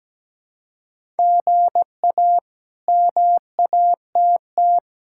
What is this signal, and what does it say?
Morse code, heard as a keyed tone, about 17 wpm, 700 Hz. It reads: GA MATT
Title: K1USN SST QSO Practice - 17wpm